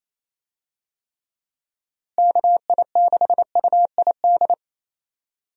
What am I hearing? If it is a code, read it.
Morse code, heard as a keyed tone, about 28 wpm, 700 Hz. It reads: KI6UID
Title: USA Call Signs - 6 Characters - Repeated 2x - 28wpm